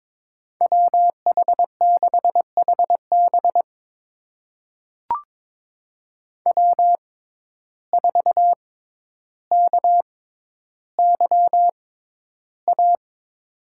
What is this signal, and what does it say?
Morse code, heard as a keyed tone, about 23 wpm, spaced as for 11 wpm, 700 Hz. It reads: WH6HB W4KYA
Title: USA Call Signs - 5 Characters - 6x Character Spacing - 22wpm